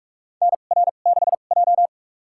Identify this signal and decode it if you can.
Morse code, heard as a keyed tone, about 47 wpm, spaced as for 33 wpm, 700 Hz. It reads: NR6J
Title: USA Call Signs - 4 Characters - 2x Character Spacing - 45wpm